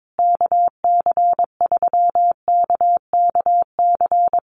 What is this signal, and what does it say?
Morse code, heard as a keyed tone, about 22 wpm, 700 Hz. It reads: KC3KKC
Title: USA Call Signs Repeated 3x 22wpm